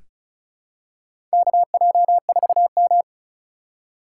Morse code, heard as a keyed tone, about 34 wpm, 700 Hz. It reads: KJ4M
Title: USA Call Signs - 4 Characters - 3x Character Spacing - 35wpm